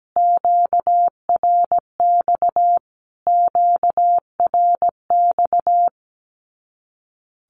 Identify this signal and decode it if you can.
Morse code, heard as a keyed tone, about 17 wpm, 700 Hz. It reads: QRX QRX
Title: Q-Codes Repeated 3x 17wpm